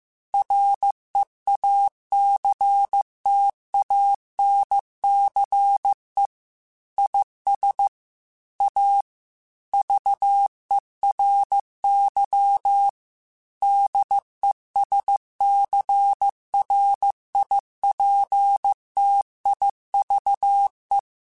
Morse code, heard as a keyed tone, about 15 wpm, 800 Hz. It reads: REACTANCE IS A VERY DESCRIPTIVE
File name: Technical Subjects / Technical Subjects-mp3s / Reactance @13